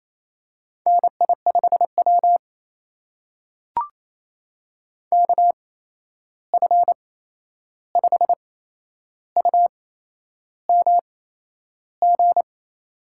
Morse code, heard as a keyed tone, about 28 wpm, spaced as for 11 wpm, 700 Hz. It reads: NI5W KF5UMG
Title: USA Call Signs - 8x Character Spacing - 28wpm